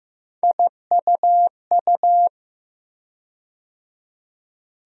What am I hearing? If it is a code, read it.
Morse code, heard as a keyed tone, about 15 wpm, 700 Hz. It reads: IUU
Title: USA Call Sign Postfixes - 3 Letters 15wpm